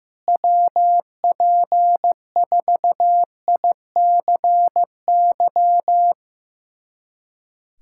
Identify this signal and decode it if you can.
Morse code, heard as a keyed tone, about 16 wpm, 700 Hz. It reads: WP4ICY
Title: USA Callsign DE Callsign - 6 Characters - 15wpm